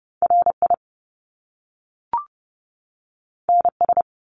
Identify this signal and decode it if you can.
Morse code, heard as a keyed tone, about 30 wpm, 700 Hz. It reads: RI NS
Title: Top Two Letter Combinations in Words 30wpm